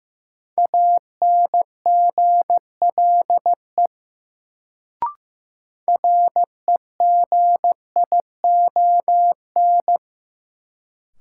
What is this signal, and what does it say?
Morse code, heard as a keyed tone, about 15 wpm, 700 Hz. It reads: ANGLE REGION
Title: Sentences from Top 1500 Words - New Word Drill Alt - 15wpm